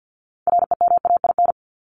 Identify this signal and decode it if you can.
Morse code, heard as a keyed tone, about 50 wpm, 700 Hz. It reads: REMAIN
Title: ES:S Suffix - Top 100 Words with Base Words 50wpm